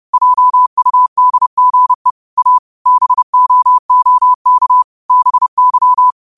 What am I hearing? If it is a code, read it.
Morse code, heard as a keyed tone, about 30 wpm, 1000 Hz. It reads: JUDGE A BOOK BY